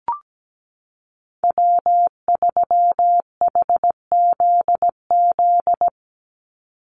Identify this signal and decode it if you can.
Morse code, heard as a keyed tone, about 17 wpm, 700 Hz. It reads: W3HZZ
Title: Call Signs Repeated 3x - 17wpm